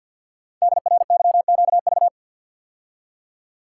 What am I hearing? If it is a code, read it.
Morse code, heard as a keyed tone, about 52 wpm, 700 Hz. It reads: DRYQV
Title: Code Groups - 5 Letters - 50wpm